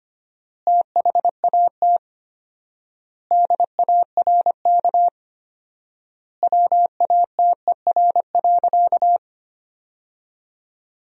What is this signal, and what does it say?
Morse code, heard as a keyed tone, about 25 wpm, 700 Hz. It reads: THAT DARK WATER.
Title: 4x Word-Spacing - Sentences from Top 500 Words - 25wpm